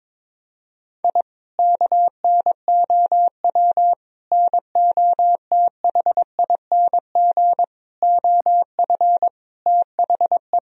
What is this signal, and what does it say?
Morse code, heard as a keyed tone, about 22 wpm, 700 Hz. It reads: I KNOW NOTHING OF THE
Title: Sentences from Top 1000 Words - Review Part 3 - 22wpm